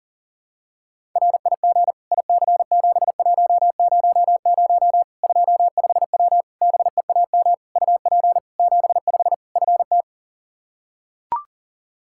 Morse code, heard as a keyed tone, about 40 wpm, 700 Hz. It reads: RIG IC7100 25W BEAM UP 75 FT
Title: Practice QSO - Exchange 2 - Info - 40wpm